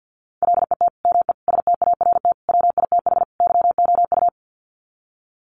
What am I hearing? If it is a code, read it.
Morse code, heard as a keyed tone, about 50 wpm, 700 Hz. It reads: LET ME START WITH YOU
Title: Sets of 5 Words Encore 50wpm